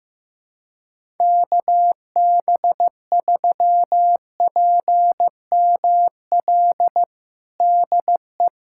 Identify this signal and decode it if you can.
Morse code, heard as a keyed tone, about 15 wpm, 700 Hz. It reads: KB3PML DE
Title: USA Callsign DE Callsign - 6 Characters - 15wpm